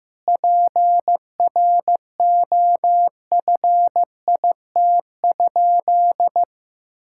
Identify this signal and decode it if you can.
Morse code, heard as a keyed tone, about 15 wpm, 700 Hz. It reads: PROFIT?
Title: Sentences from Top 1000 Words - Review Part 5 - 15wpm